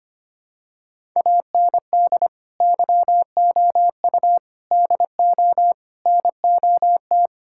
Morse code, heard as a keyed tone, about 25 wpm, 700 Hz. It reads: AND YOU DO NOT
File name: practice / 25wpm / Sentences with 4-Word Phrases - 25wpm